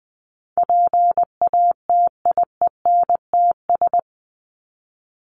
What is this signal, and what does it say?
Morse code, heard as a keyed tone, about 19 wpm, 700 Hz. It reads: PATIENTS
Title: ES:S Suffix - Top 100 Words with Base Words 20wpm